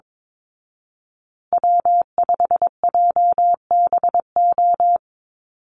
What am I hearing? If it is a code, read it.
Morse code, heard as a keyed tone, about 22 wpm, 700 Hz. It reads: W5JBO
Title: Call Signs Repeated 2x - 22wpm